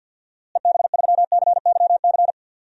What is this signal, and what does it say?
Morse code, heard as a keyed tone, about 51 wpm, 700 Hz. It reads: EB3CYC
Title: DXCC-100 Call Signs - 6 Characters - Repeated 3x - 50wpm